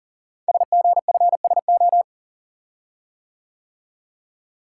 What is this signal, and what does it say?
Morse code, heard as a keyed tone, about 38 wpm, 700 Hz. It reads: SGFSO
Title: Code Groups - 5 Letters - 40wpm